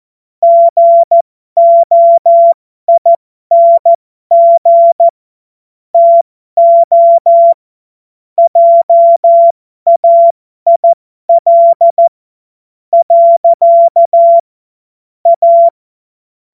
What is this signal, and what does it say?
Morse code, heard as a keyed tone, about 14 wpm, spaced as for 11 wpm, 680 Hz. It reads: GOING TO JAIL. A